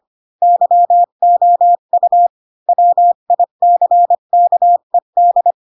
Morse code, heard as a keyed tone, about 25 wpm, 700 Hz. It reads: YOU WICKED
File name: Pinocchio / Pinocchio ch3 20 25 ik1hge